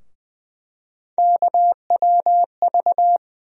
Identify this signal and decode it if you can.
Morse code, heard as a keyed tone, about 20 wpm, 700 Hz. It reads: KWV
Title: USA Call Sign Postfixes - 3 Letters 20wpm